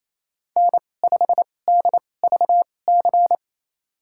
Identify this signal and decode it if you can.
Morse code, heard as a keyed tone, about 28 wpm, spaced as for 20 wpm, 700 Hz. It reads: N5DVC
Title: USA Call Signs - 5 Characters - 2x Character Spacing - 28wpm